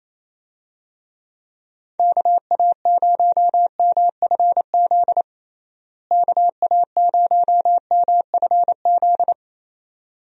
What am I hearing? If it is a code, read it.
Morse code, heard as a keyed tone, about 28 wpm, 700 Hz. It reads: KA0MFZ KA0MFZ
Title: USA Call Signs - 6 Characters - Repeated 3x - 28wpm